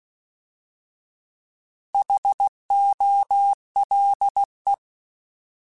Morse code, heard as a keyed tone, about 16 wpm, 800 Hz. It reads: HOLE
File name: Alphabetical / Alphabetical mp3s / Hs @15